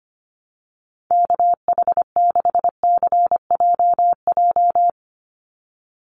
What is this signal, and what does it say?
Morse code, heard as a keyed tone, about 25 wpm, 700 Hz. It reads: KH6CJJ
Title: Call Signs Repeated 2x - 25wpm